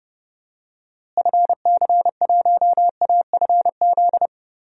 Morse code, heard as a keyed tone, about 30 wpm, 700 Hz. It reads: FC1AFZ
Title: DXCC-100 Call Signs - 6 Characters - Repeated 2x - 30wpm